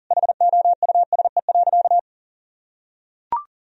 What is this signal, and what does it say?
Morse code, heard as a keyed tone, about 40 wpm, 700 Hz. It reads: HOUSE.
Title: Sentences from Top 1000 Words - Review Part 1 - 40wpm